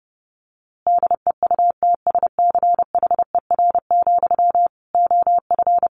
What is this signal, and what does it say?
Morse code, heard as a keyed tone, about 30 wpm, 700 Hz. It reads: DEUTSCHER, OF
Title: A Sound Of Thunder 30wpm